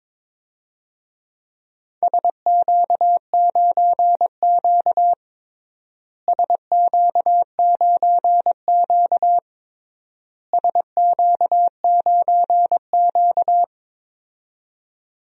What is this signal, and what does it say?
Morse code, heard as a keyed tone, about 22 wpm, 700 Hz. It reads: SQ9Q SQ9Q SQ9Q
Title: DXCC-100 Call Signs - 4 Characters - Repeated 3x - 22wpm